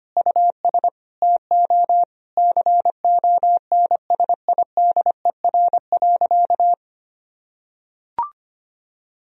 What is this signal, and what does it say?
Morse code, heard as a keyed tone, about 25 wpm, 700 Hz. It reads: US TO CONSIDER.
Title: Sentences from Top 1000 Words - Review Part 4 - 25wpm